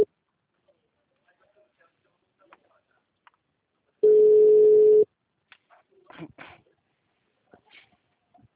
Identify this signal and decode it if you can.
Ringback tone (Europe, 425 Hz).
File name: out- -2020- - - .85896